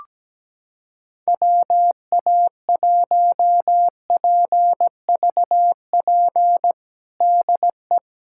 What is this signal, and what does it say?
Morse code, heard as a keyed tone, about 17 wpm, 700 Hz. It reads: WA1PVP DE
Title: USA Callsign DE Callsign - 6 Characters - 17wpm